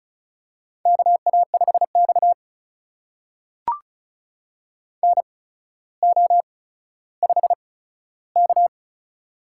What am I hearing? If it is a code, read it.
Morse code, heard as a keyed tone, about 36 wpm, spaced as for 15 wpm, 700 Hz. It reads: KA5X NO5K
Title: USA Call Signs - 4 Characters - 8x Character Spacing - 35wpm